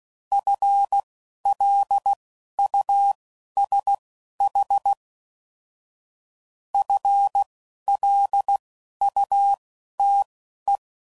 Morse code, heard as a keyed tone, about 16 wpm, spaced as for 11 wpm, 800 Hz. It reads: FLUSH FLUTE